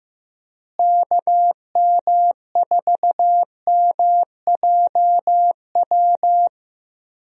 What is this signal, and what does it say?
Morse code, heard as a keyed tone, about 15 wpm, 700 Hz. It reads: KM4MJW
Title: USA Call Signs - 6 Characters - Repeated 3x - 15wpm